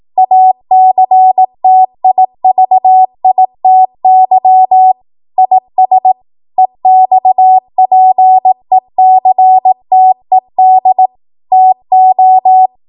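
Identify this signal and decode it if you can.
Morse code, heard as a keyed tone, about 18 wpm, 750 Hz. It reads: ACTIVITY IS EXPECTED TO
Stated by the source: ARRL 18 wpm Morse code practice archive transmission.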